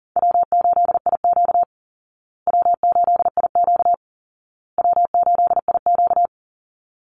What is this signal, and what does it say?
Morse code, heard as a keyed tone, about 40 wpm, 700 Hz. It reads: W8IQ W8IQ W8IQ
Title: Call Signs Repeated 2x - 40wpm